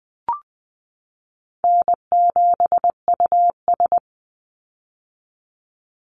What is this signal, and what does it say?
Morse code, heard as a keyed tone, about 20 wpm, 700 Hz. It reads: N7US
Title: Easy Call Signs - 20wpm